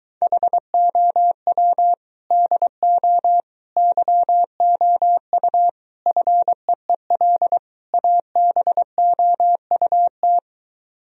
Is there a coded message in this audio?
Yes: Morse code from 0.2 to 10.4 s, heard as a keyed tone, about 23 wpm, 700 Hz. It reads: HOW DO YOU FEEL ABOUT